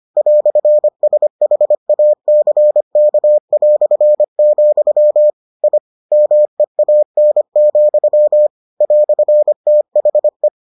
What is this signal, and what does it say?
Morse code, heard as a keyed tone, about 25 wpm, 600 Hz. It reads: "SHACK", I MEAN, "THE